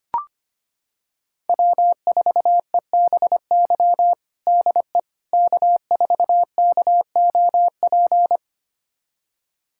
Morse code, heard as a keyed tone, about 25 wpm, 700 Hz. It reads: W4EBY DE K4KOP
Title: USA Callsign DE Callsign - 5 Characters - 25wpm